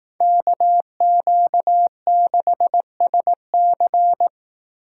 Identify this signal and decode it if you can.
Morse code, heard as a keyed tone, about 18 wpm, 700 Hz. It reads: KQ6SC
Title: Speed Racing - USA 2x2 Call Signs 15wpm